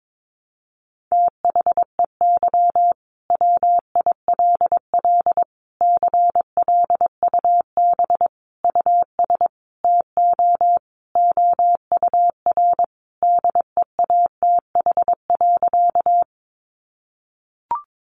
Morse code, heard as a keyed tone, about 22 wpm, 700 Hz. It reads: THEY WILL CLUB US TO OUR DEATH.
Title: Sentences from Top 1200 Words 22wpm